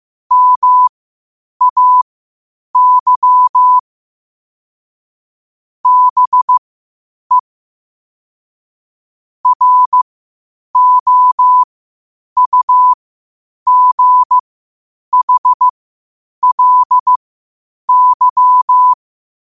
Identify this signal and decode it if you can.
Morse code, heard as a keyed tone, about 15 wpm, spaced as for 8 wpm, 1000 Hz. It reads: MAY BE ROUGHLY
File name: qotdfiles / 8wpm